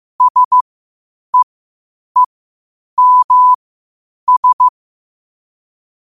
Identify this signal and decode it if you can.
Morse code, heard as a keyed tone, about 15 wpm, spaced as for 8 wpm, 1000 Hz. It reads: SEEMS
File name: qotdfiles / 8wpm